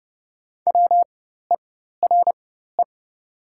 Morse code, heard as a keyed tone, about 31 wpm, spaced as for 14 wpm, 700 Hz. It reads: WERE
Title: 4x Character Spacing - Top 300 Words - 30wpm